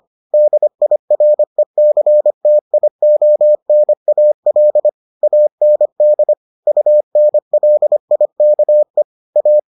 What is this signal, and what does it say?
Morse code, heard as a keyed tone, about 25 wpm, 600 Hz. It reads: DIRECTIONAL AND UNLIKE A